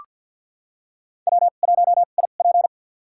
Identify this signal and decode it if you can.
Morse code, heard as a keyed tone, about 53 wpm, spaced as for 37 wpm, 700 Hz. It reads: W1IP
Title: USA Call Signs - 4 Characters - 2x Character Spacing - 50wpm